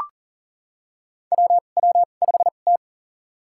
Morse code, heard as a keyed tone, about 40 wpm, spaced as for 29 wpm, 700 Hz. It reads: WW5T
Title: USA Call Signs - 4 Characters - 2x Character Spacing - 40wpm